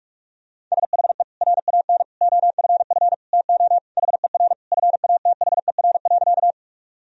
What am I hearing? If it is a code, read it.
Morse code, heard as a keyed tone, about 46 wpm, 700 Hz. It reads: SHE RAN OFF TO HER FATHER.
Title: Sentences from Top 1000 Words - Review Part 3 - 45wpm